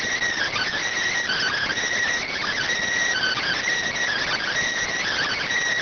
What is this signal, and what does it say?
Emergency Alert System (SAME) tones (header bursts), heard as harsh buzzing bursts. It is heard from 1.2 to 5.5 s.